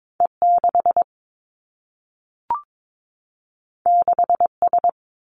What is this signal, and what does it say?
Morse code, heard as a keyed tone, about 22 wpm, 700 Hz. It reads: E 6 6 S
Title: Two Challenging Letter-Number Combinations 22wpm